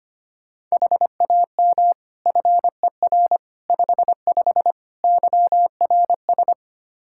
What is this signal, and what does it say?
Morse code, heard as a keyed tone, about 25 wpm, 700 Hz. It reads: HAM FER 55 YRS
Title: Practice QSO - Exchange 3 - More Info - 25wpm